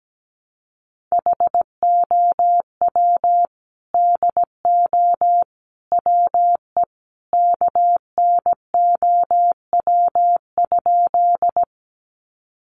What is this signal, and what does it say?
Morse code, heard as a keyed tone, about 17 wpm, 700 Hz. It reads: HOW DO WE KNOW?
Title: A Sound Of Thunder 17wpm